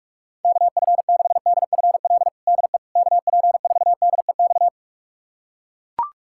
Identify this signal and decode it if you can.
Morse code, heard as a keyed tone, about 43 wpm, 700 Hz. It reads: KF6DFL DE KP4DEX
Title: USA Callsign DE Callsign - 6 Characters - 45wpm